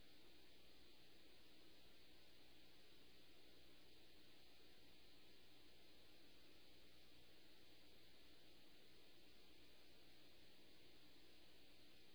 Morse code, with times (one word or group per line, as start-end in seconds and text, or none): none